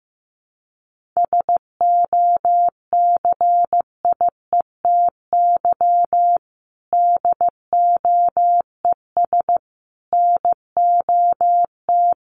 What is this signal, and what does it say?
Morse code, heard as a keyed tone, about 14 wpm, 700 Hz. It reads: SOCIETY DOES NOT
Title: Sentences from Top 1400 Words 15wpm